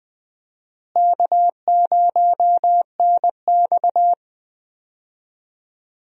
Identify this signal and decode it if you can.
Morse code, heard as a keyed tone, about 20 wpm, 700 Hz. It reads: K0NX
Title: USA Call Signs - 4 Characters - Repeated 2x - 20wpm